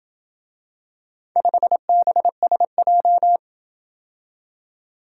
Morse code, heard as a keyed tone, about 27 wpm, 700 Hz. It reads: 5BSJ